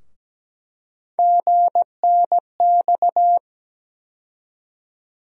Morse code, heard as a keyed tone, about 17 wpm, 700 Hz. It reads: GNX
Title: USA Call Sign Postfixes - 3 Letters 17wpm